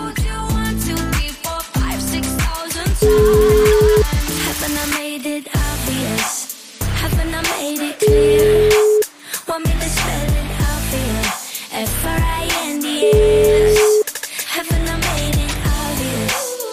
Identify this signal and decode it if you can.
Ringback tone (Europe, 425 Hz), with noise.